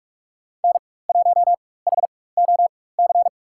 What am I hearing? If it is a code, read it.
Morse code, heard as a keyed tone, about 42 wpm, spaced as for 20 wpm, 700 Hz. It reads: N1HOC
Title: USA Call Signs - 5 Characters - 4x Character Spacing - 45wpm